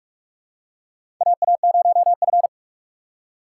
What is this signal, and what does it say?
Morse code, heard as a keyed tone, about 44 wpm, 700 Hz. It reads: AA0F
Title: USA Call Signs - 4 Characters - 6x Character Spacing - 45wpm